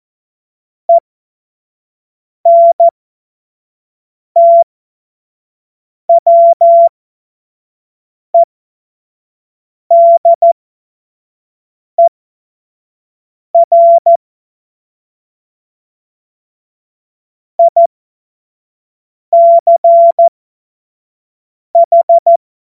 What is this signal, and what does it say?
Morse code, heard as a keyed tone, about 14 wpm, spaced as for 5 wpm, 680 Hz. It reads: ENTWEDER ICH
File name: Morsekurs / Witze / Witz 10. Januar 2023 25 F70